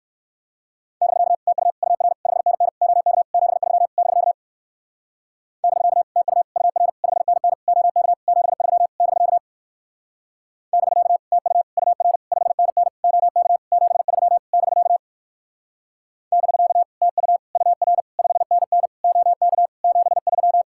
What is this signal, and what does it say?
Morse code, heard as a keyed tone, about 59 wpm, 700 Hz. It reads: <BK> TU UR 5NN OK 73 <BK> <BK> TU UR 5NN OK 73 <BK> <BK> TU UR 5NN OK 73 <BK> <BK> TU UR 5NN OK 73